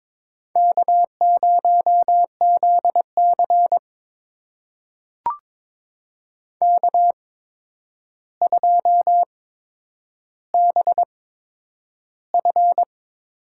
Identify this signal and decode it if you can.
Morse code, heard as a keyed tone, about 22 wpm, spaced as for 9 wpm, 700 Hz. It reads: K0ZC K2BF
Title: USA Call Signs - 4 Characters - 8x Character Spacing - 22wpm